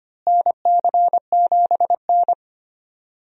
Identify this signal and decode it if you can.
Morse code, heard as a keyed tone, about 25 wpm, 700 Hz. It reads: NC7N
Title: USA Call Signs - 4 Characters - Repeated 3x - 25wpm